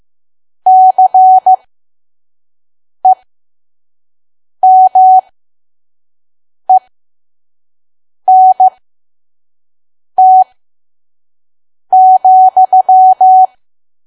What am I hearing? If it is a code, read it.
Morse code, heard as a keyed tone, about 15 wpm, spaced as for 5 wpm, 750 Hz. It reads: CEMENT,
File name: Archive / 5 WPM files / PM